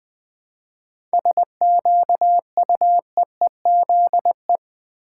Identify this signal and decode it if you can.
Morse code, heard as a keyed tone, about 20 wpm, 700 Hz. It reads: SQUEEZE
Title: 7 Letter Words - First 3 Letters Given - 20wpm